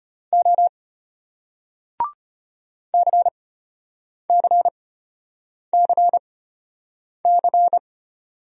Morse code, heard as a keyed tone, about 34 wpm, spaced as for 8 wpm, 700 Hz. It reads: O CCCC